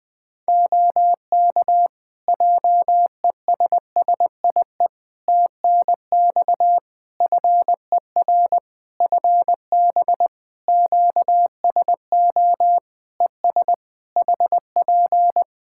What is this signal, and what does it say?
Morse code, heard as a keyed tone, about 20 wpm, 700 Hz. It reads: OK JESSIE TNX FER FB QSO ES HP
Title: Practice QSO - Exchange 4 - Ending - 20wpm